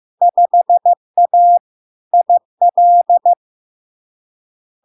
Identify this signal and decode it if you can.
Morse code, heard as a keyed tone, about 15 wpm, 700 Hz. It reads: 5A IL
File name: LARK-FD-2022-Log-in-CW-15wpm-700-Hz